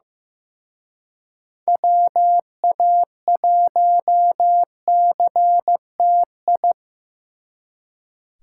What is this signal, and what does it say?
Morse code, heard as a keyed tone, about 18 wpm, 700 Hz. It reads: WA1CTI
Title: Speed Racing - USA Call Signs 15wpm